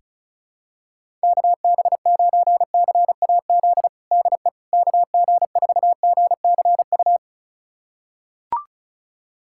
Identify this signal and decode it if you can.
Morse code, heard as a keyed tone, about 34 wpm, 700 Hz. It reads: KB9CAZ DE KG4GCU
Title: USA Callsign DE Callsign 35wpm